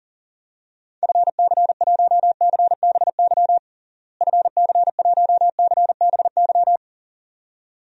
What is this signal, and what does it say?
Morse code, heard as a keyed tone, about 40 wpm, 700 Hz. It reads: FC1CBY FC1CBY
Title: DXCC-100 Call Signs - 6 Characters - Repeated 2x - 40wpm